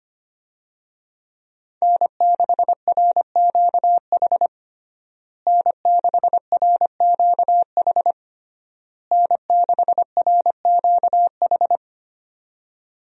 Morse code, heard as a keyed tone, about 25 wpm, 700 Hz. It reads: N6RQH N6RQH N6RQH
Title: USA Call Signs - 5 Characters - Repeated 3x - 25wpm